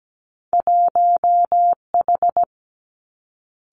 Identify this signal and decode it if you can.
Morse code, heard as a keyed tone, about 17 wpm, 700 Hz. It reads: 1H
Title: Two Challenging Letter-Number Combinations 17wpm